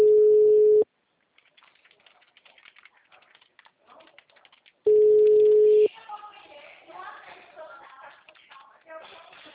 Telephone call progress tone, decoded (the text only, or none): ringback tone